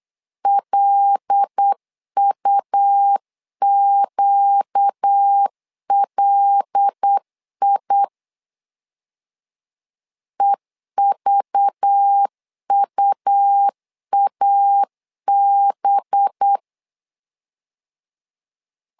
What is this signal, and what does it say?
Morse code, heard as a keyed tone, about 8 wpm, 800 Hz. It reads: LUQLI EVUAB
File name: LEZIONI / 5 LBVC / 5 LBV 8wpm 2837